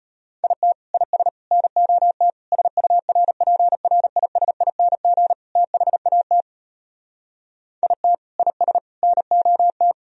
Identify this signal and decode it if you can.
Morse code, heard as a keyed tone, about 33 wpm, 700 Hz. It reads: IT IS NOT SURPRISING THAT IT IS NOT